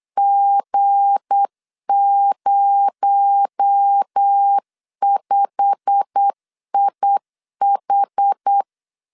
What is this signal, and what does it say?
Morse code, heard as a keyed tone, about 8 wpm, 800 Hz. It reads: G05IH
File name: LEZIONI / 9 2867 8wpm 3440